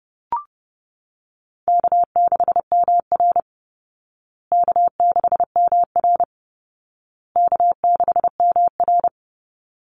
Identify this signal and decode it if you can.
Morse code, heard as a keyed tone, about 30 wpm, 700 Hz. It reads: K6MR K6MR K6MR
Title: Easy Call Signs Repeated 3x - 30wpm